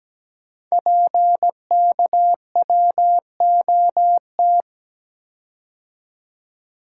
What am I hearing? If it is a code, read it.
Morse code, heard as a keyed tone, about 17 wpm, 700 Hz. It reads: PKWOT
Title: Code Groups - 5 Letters - 17wpm